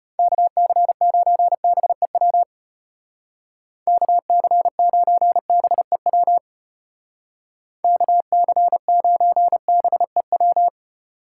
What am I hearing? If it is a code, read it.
Morse code, heard as a keyed tone, about 30 wpm, 700 Hz. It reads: KC9BEW KC9BEW KC9BEW
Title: Speed Racing - USA 2x3 Call Signs 25wpm